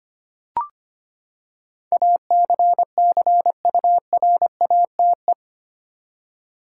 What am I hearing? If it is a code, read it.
Morse code, heard as a keyed tone, about 25 wpm, 700 Hz. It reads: ACCURATE
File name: Eight Letter Words - 25wpm